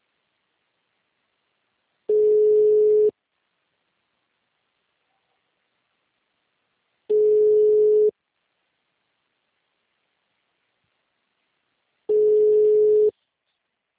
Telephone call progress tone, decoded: ringback tone

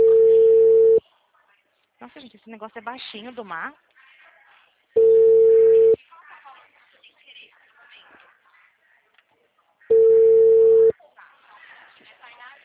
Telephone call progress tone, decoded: ringback tone

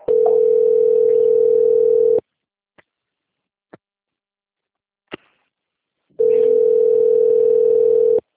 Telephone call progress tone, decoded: ringback tone

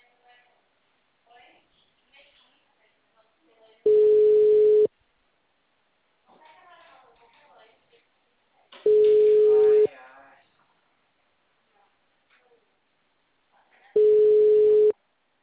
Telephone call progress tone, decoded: ringback tone

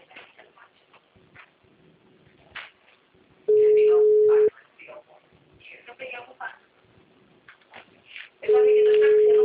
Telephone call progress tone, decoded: ringback tone

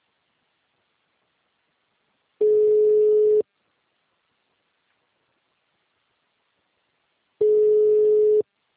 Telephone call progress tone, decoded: ringback tone

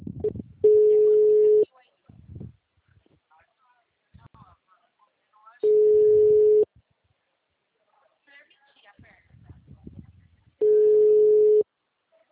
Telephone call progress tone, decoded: ringback tone

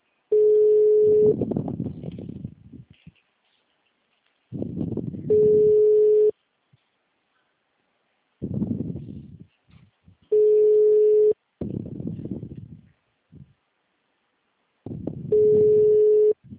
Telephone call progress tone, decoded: ringback tone

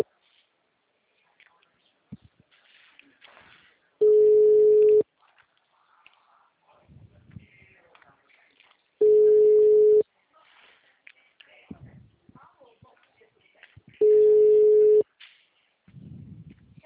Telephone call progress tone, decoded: ringback tone

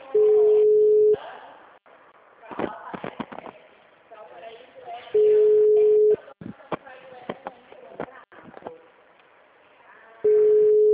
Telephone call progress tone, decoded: ringback tone

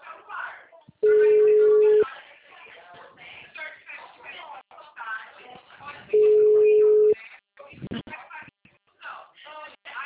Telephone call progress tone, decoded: ringback tone

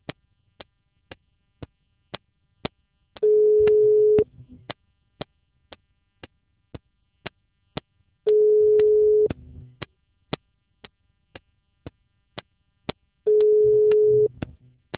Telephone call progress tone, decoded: ringback tone

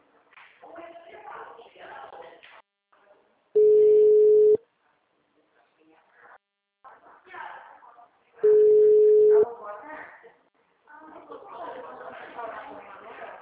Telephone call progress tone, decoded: ringback tone

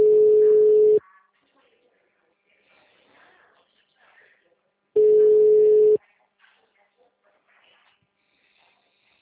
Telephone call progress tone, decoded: ringback tone

